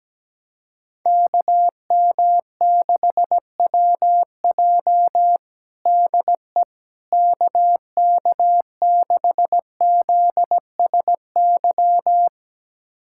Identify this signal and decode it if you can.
Morse code, heard as a keyed tone, about 17 wpm, 700 Hz. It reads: KM6WJ DE KK6ZSY
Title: USA Callsign DE Callsign 17wpm